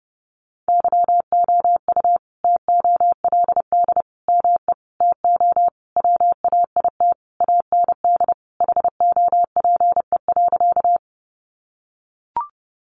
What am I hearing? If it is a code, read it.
Morse code, heard as a keyed tone, about 30 wpm, 700 Hz. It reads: YOU TOLD ME TO WAIT AND HOPE.
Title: Sentences from Top 1000 Words 30wpm